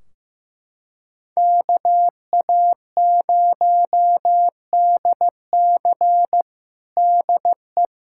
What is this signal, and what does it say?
Morse code, heard as a keyed tone, about 16 wpm, 700 Hz. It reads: KA0DC DE
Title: Practice QSO - Exchange 4 - Ending - 15wpm